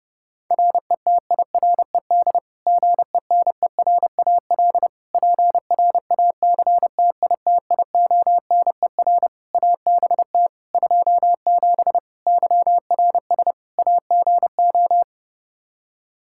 Morse code, heard as a keyed tone, about 32 wpm, 700 Hz. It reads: RETIRED GENERAL PRACTITIONER ABT 27 YRS AGO
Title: Practice QSO - Exchange 3 - More Info - 30wpm